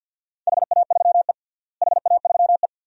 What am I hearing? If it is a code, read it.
Morse code, heard as a keyed tone, about 51 wpm, 700 Hz. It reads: HA3E HA3E
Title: DXCC-100 Call Signs - 4 Characters - Repeated 2x - 50wpm